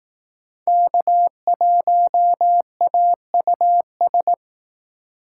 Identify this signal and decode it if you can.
Morse code, heard as a keyed tone, about 18 wpm, 700 Hz. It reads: K1AUS
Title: Speed Racing - USA 1x3 Call Signs 15wpm